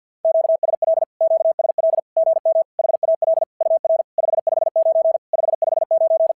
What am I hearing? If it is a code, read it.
Morse code, heard as a keyed tone, about 50 wpm, 650 Hz. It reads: QSL QSL GM HAL UR 559 559
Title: SOTA QSO Practice 50wpm